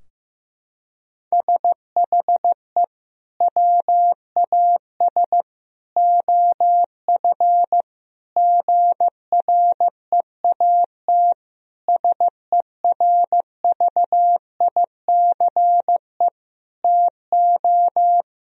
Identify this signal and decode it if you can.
Morse code, heard as a keyed tone, about 15 wpm, 700 Hz. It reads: SHE WAS OF GREAT SERVICE TO
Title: Sentences from Top 1000 Words - Review Part 4 - 15wpm